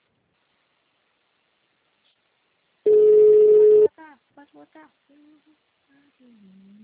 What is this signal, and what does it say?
Ringback tone (Europe, 425 Hz), with speech.